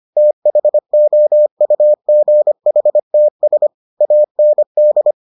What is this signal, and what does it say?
Morse code, heard as a keyed tone, about 25 wpm, 600 Hz. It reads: THOUGHTS AND